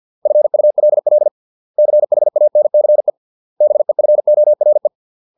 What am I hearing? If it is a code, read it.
Morse code, heard as a keyed tone, about 50 wpm, 600 Hz. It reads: FULL CHANCE BEFORE